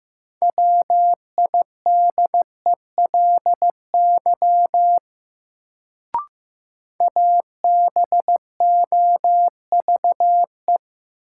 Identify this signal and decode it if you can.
Morse code, heard as a keyed tone, about 15 wpm, 700 Hz. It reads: WIDELY ABOVE
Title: Sentences from Top 1200 Words - New Word Drill Alt - 15wpm